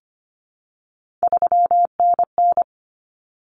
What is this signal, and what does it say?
Morse code, heard as a keyed tone, about 25 wpm, 700 Hz. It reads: 3NN
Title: Common Cut Numbers - 3-Digit 25wpm